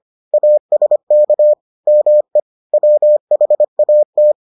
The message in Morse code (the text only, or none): ASK ME WHAT